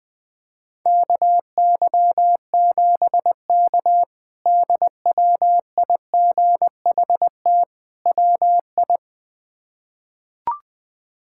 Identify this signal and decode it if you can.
Morse code, heard as a keyed tone, about 20 wpm, 700 Hz. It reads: KY7K DWIGHT WI
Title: K1USN SST QSO Practice - 20wpm